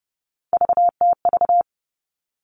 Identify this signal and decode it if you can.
Morse code, heard as a keyed tone, about 30 wpm, 700 Hz. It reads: VTV
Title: All Cut Numbers - 3-Digit 30wpm